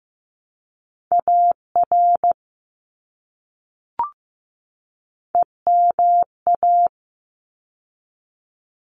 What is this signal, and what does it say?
Morse code, heard as a keyed tone, about 15 wpm, 700 Hz. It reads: AR EMA
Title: ARRL Sections 15wpm